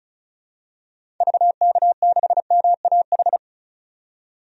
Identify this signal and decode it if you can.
Morse code, heard as a keyed tone, about 34 wpm, 700 Hz. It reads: VK6MAH
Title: DXCC-100 Call Signs - 6 Characters - 35wpm